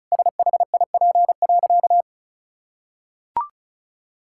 Morse code, heard as a keyed tone, about 34 wpm, 700 Hz. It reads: SHIP.
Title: Sentences from Top 1000 Words - Review Part 5 - 35wpm